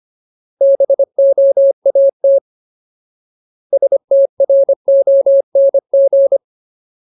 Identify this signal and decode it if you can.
Morse code, heard as a keyed tone, about 25 wpm, 550 Hz. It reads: BOAT STRONG
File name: Ultimate Word List / 25WPM Ultimate